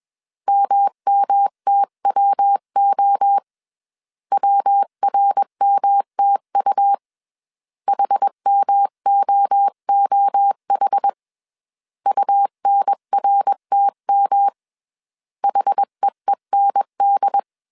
Morse code, heard as a keyed tone, about 21 wpm, 800 Hz. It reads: MMTWO WRMTU HMOOH UNRTM HEEND